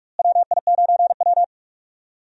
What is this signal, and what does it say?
Morse code, heard as a keyed tone, about 46 wpm, 700 Hz. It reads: WI9W
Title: USA Call Signs - 4 Characters - Repeated 3x - 45wpm